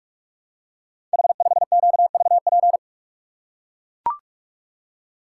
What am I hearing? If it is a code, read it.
Morse code, heard as a keyed tone, about 44 wpm, 700 Hz. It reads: H5QVP
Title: Code Groups - 5 Letter:Numbers - 45wpm